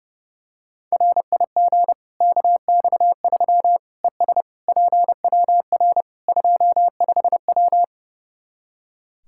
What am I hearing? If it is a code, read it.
Morse code, heard as a keyed tone, about 30 wpm, 700 Hz. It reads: RIG KX3 ES PWR 25W
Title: K1USN SST QSO Practice 30wpm